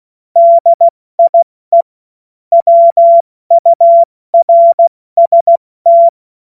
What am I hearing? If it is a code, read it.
Morse code, heard as a keyed tone, about 16 wpm, 680 Hz. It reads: DIE WURST
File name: Morsekurs / Witze / Witz 09 April 2024 70 F80